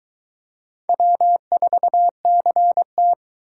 Morse code, heard as a keyed tone, about 22 wpm, 700 Hz. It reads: W4CT